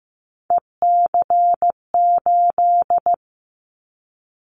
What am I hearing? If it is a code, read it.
Morse code, heard as a keyed tone, about 15 wpm, 700 Hz. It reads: EC8
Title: Callsign Prefixes for Top 50 DXCC - 15wpm